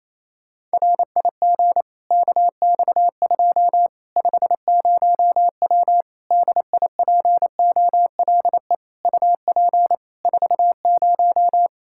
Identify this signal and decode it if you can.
Morse code, heard as a keyed tone, about 27 wpm, 700 Hz. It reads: RIG KX2 50W DIPOLE UP 40
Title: Practice QSO - Exchange 2 - Info - 28wpm